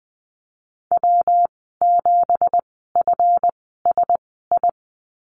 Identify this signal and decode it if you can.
Morse code, heard as a keyed tone, about 20 wpm, spaced as for 15 wpm, 700 Hz. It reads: W7FSI
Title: 200 Lesson 32g - Letter k - Call Signs 20wpm